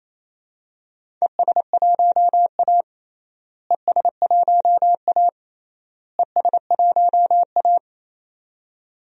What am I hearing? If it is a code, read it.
Morse code, heard as a keyed tone, about 27 wpm, 700 Hz. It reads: ES1A ES1A ES1A
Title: DXCC-100 Call Signs - 4 Characters - Repeated 3x - 28wpm